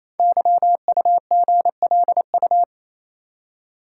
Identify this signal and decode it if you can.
Morse code, heard as a keyed tone, about 27 wpm, 700 Hz. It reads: YUGLU
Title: Code Groups - 5 Letter:Numbers - 28wpm